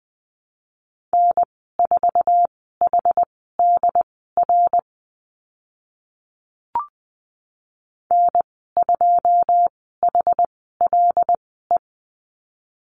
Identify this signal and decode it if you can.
Morse code, heard as a keyed tone, about 20 wpm, spaced as for 14 wpm, 700 Hz. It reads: N4HDR N2HLE
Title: 088 Lesson 15g - Number 5 - Call Signs 20wpm